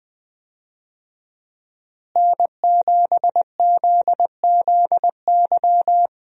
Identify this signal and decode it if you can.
Morse code, heard as a keyed tone, about 21 wpm, 700 Hz. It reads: N7ZZY
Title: USA Call Signs - 5 Characters - Repeated 3x - 20wpm